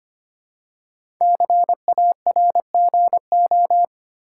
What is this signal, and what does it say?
Morse code, heard as a keyed tone, about 26 wpm, 700 Hz. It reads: CARGO
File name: 5 Letter Words - First 3 Letters Given - 25wpm